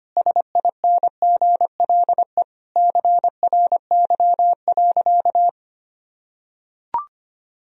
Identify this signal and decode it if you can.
Morse code, heard as a keyed tone, about 25 wpm, 700 Hz. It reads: SINGLE CRY.
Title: Sentences from Top 1000 Words - Review Part 4 - 25wpm